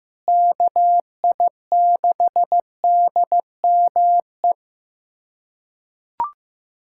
Morse code, heard as a keyed tone, about 16 wpm, 700 Hz. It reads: KI6DME
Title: USA Callsign DE Callsign - 6 Characters - 15wpm